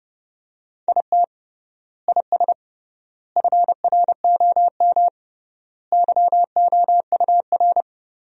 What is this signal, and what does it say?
Morse code, heard as a keyed tone, about 32 wpm, 700 Hz. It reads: IT IS FROM YOUR
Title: 3x Word-Spacing - Sentences from Top 300 Words - 30wpm